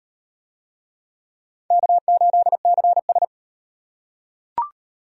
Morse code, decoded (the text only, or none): K8CS